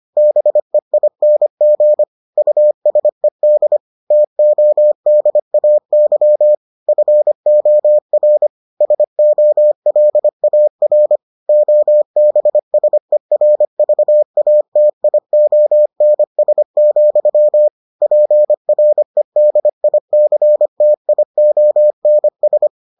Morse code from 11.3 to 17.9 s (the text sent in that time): OBSERVATIONS,